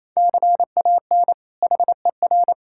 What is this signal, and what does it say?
Morse code, heard as a keyed tone, about 28 wpm, 700 Hz. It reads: CAN HER
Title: Sentences from Top 1000 Words - Review Part 5 - 28wpm